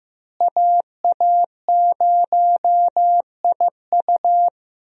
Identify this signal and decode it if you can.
Morse code, heard as a keyed tone, about 15 wpm, 700 Hz. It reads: AA0IU
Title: USA Call Signs - 5 Characters - Repeated 2x - 15wpm